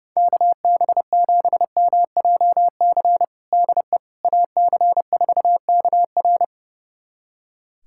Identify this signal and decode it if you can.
Morse code, heard as a keyed tone, about 30 wpm, 700 Hz. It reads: KB7MJC DE AC4KR
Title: USA Callsign DE Callsign 30wpm